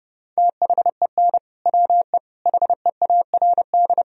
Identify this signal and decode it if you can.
Morse code, heard as a keyed tone, about 30 wpm, 700 Hz. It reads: THEN WE HEARD
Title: Sentences from Top 1000 Words - Review Part 2 - 30wpm